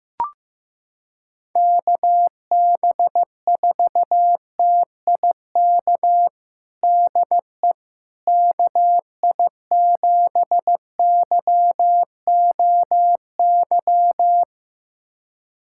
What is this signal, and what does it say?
Morse code, heard as a keyed tone, about 16 wpm, 700 Hz. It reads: KB4TIK DE KI7YOY
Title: USA Callsign DE Callsign - 6 Characters - 15wpm